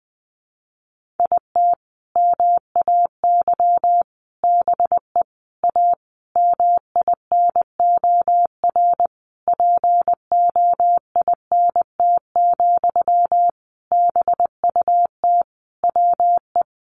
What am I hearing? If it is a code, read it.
Morse code, heard as a keyed tone, about 20 wpm, 700 Hz. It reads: IT MAY BE A MINOR POINT, BUT WE
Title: Sentences from Top 1500 Words 20wpm